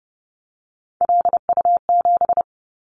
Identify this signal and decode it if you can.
Morse code, heard as a keyed tone, about 30 wpm, 700 Hz. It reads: LU7
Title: Callsign Prefixes for Top 100 DXCC - 30wpm